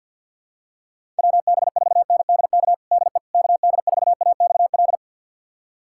Morse code, heard as a keyed tone, about 46 wpm, 700 Hz. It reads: WB4NDK DE KD4AXL
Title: USA Callsign DE Callsign - 6 Characters - 50wpm